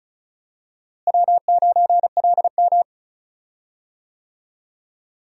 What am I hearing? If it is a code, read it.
Morse code, heard as a keyed tone, about 34 wpm, 700 Hz. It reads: W9LM
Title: USA Call Signs - 4 Characters - Repeated 2x - 35wpm